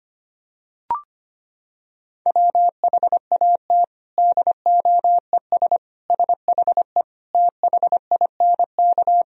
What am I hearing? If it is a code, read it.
Morse code, heard as a keyed tone, about 25 wpm, 700 Hz. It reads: WHAT DOES SHE THINK
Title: Sentences from Top 200 Words and Abbreviations 25wpm